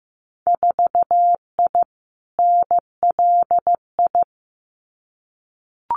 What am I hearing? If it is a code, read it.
Morse code, heard as a keyed tone, about 15 wpm, 700 Hz. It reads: 4I NLI
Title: Winter Field Day Exchanges Encore 15wpm